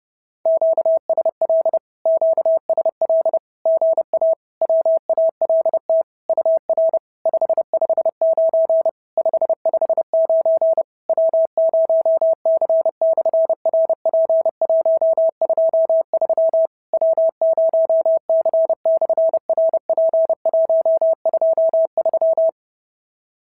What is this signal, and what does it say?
Morse code, heard as a keyed tone, about 30 wpm, 650 Hz. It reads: QSL QSL GA WALT UR 559 559 W0C/RP123 W0C/RP123
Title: SOTA QSO Practice 30wpm